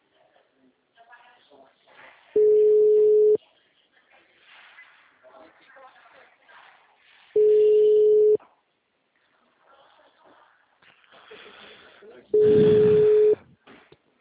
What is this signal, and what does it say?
Ringback tone (Europe, 425 Hz), with speech.